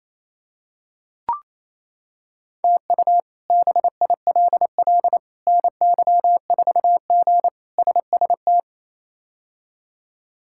Morse code, heard as a keyed tone, about 28 wpm, 700 Hz. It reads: TU BILL NY4G SST
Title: K1USN SST QSO Practice - 28wpm